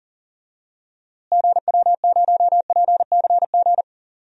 Morse code, heard as a keyed tone, about 40 wpm, 700 Hz. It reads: GW0PCG
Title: DXCC-100 Call Signs - 6 Characters - Repeated 3x - 40wpm